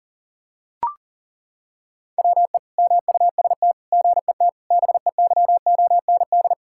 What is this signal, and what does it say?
Morse code, heard as a keyed tone, about 40 wpm, 690 Hz. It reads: WE MUST GET BEYOND
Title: Sentences from Top 1000 Words - Review Part 6 - 40wpm